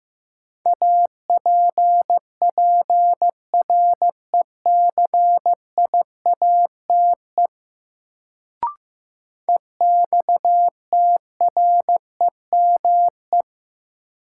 Morse code, heard as a keyed tone, about 15 wpm, 700 Hz. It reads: APPRECIATE EXTREME
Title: Sentences from Top 1300 Words - New Word Drill Alt - 15wpm